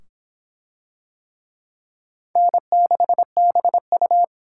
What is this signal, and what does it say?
Morse code, heard as a keyed tone, about 24 wpm, 700 Hz. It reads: N6BU